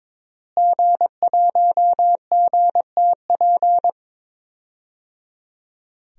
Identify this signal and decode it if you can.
Morse code, heard as a keyed tone, about 22 wpm, 700 Hz. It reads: G1GTP
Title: DXCC-100 Call Signs - 5 Characters - Repeated 3x - 22wpm